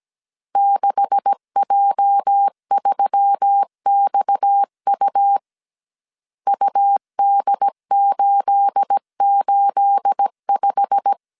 Morse code, heard as a keyed tone, about 17 wpm, 800 Hz. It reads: 6J3XU UD885
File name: CW / DET / 1 9 1500C S A 16wpm 2828